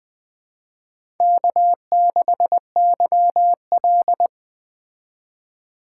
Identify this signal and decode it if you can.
Morse code, heard as a keyed tone, about 20 wpm, 700 Hz. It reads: K6YL
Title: Speed Racing - USA 1x2 Call Signs 17wpm